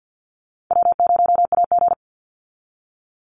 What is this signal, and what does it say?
Morse code, heard as a keyed tone, about 50 wpm, 700 Hz. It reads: W0AG
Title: Call Signs - 50wpm